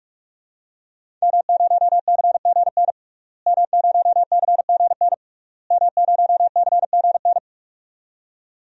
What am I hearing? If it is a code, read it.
Morse code, heard as a keyed tone, about 46 wpm, 700 Hz. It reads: M0CGN M0CGN M0CGN
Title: DXCC-100 Call Signs - 5 Characters - Repeated 3x - 45wpm